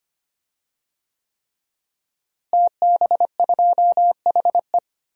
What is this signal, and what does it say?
Morse code, heard as a keyed tone, about 25 wpm, 700 Hz. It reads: TB2HE
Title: DXCC-100 Call Signs - 5 Characters - Repeated 3x - 25wpm